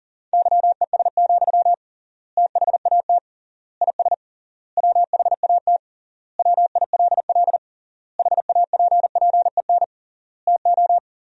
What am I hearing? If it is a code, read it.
Morse code, heard as a keyed tone, about 40 wpm, 700 Hz. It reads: YES, THAT IS WHAT WILL HAPPEN TO
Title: 3x Word-Spacing - Sentences from Top 500 Words - 40wpm